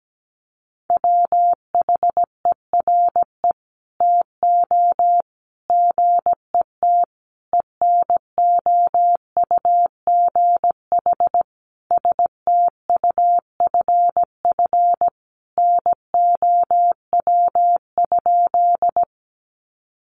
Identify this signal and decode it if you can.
Morse code, heard as a keyed tone, about 17 wpm, 700 Hz. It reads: WHERE TO GET ENOUGH STUFF NOW?
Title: Sentences from Top 1400 Words 17wpm